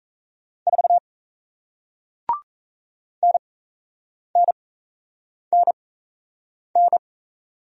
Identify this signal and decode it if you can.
Morse code, heard as a keyed tone, about 38 wpm, spaced as for 8 wpm, 700 Hz. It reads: 4 NNNN